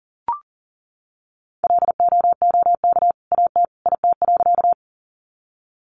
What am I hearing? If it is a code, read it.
Morse code, heard as a keyed tone, about 40 wpm, 700 Hz. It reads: LOOK AT IT.
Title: Sentences from Top 100 Words 40wpm